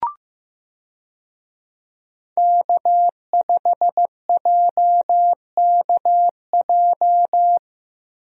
Morse code, heard as a keyed tone, about 15 wpm, 700 Hz. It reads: K5JKJ
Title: USA Call Signs - 5 Characters - Repeated 3x - 15wpm